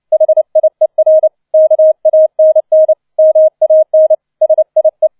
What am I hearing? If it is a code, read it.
Morse code, heard as a keyed tone, about 29 wpm, 630 Hz. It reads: HIER KANN MAN SIE
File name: 007 Letzter Streich 30wpm